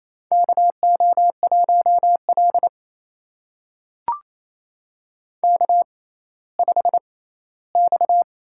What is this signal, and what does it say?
Morse code, heard as a keyed tone, about 28 wpm, spaced as for 16 wpm, 700 Hz. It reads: KO1L K5X
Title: USA Call Signs - 4 Characters - 6x Character Spacing - 28wpm